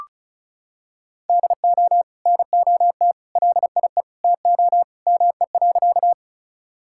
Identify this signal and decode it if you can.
Morse code, heard as a keyed tone, about 34 wpm, 700 Hz. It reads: DO NOT LIE TO ME.
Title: Sentences from Top 1600 Words 35wpm